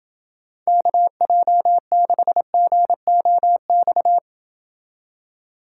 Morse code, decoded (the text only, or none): KJ6GOX